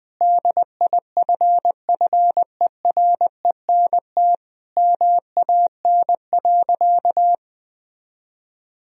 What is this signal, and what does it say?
Morse code, heard as a keyed tone, about 20 wpm, 700 Hz. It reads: DIFFERENT MAN.
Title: Sentences from Top 1000 Words - Review Part 3 - 20wpm